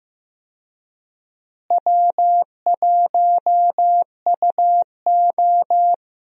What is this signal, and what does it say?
Morse code, heard as a keyed tone, about 15 wpm, 700 Hz. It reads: W1UO
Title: USA Call Signs - 4 Characters - Repeated 2x - 15wpm